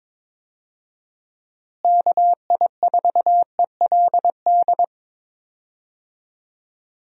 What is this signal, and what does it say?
Morse code, heard as a keyed tone, about 21 wpm, 700 Hz. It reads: KI4ELD
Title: USA Call Signs - 6 Characters - Repeated 3x - 22wpm